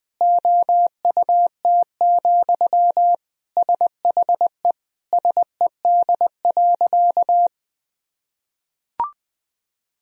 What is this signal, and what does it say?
Morse code, heard as a keyed tone, about 20 wpm, 700 Hz. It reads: OUT, SHE SED.
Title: Sentences from Top 500 Words and Abbreviations 20wpm